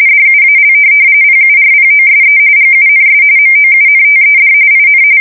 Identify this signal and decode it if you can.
RTTY, about 45.45 baud. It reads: DUAL PURPOSE GUNS MOUNTED IN
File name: fsk170 / USS-Massachusetts